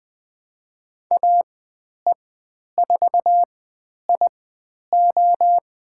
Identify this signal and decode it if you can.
Morse code, heard as a keyed tone, about 20 wpm, spaced as for 10 wpm, 700 Hz. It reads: AE4IO
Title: 095 Lesson 16g - Letter c - Call Signs 20wpm